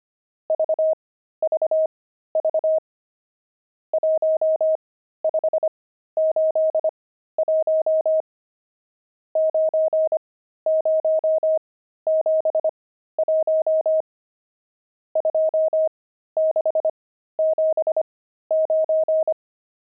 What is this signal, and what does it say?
Morse code, heard as a keyed tone, about 25 wpm, spaced as for 13 wpm, 630 Hz. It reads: VVV 1581 9071 2679
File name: pangram-fs-13-cs-25-nums7